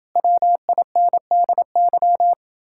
Morse code, heard as a keyed tone, about 29 wpm, 700 Hz. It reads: WINDY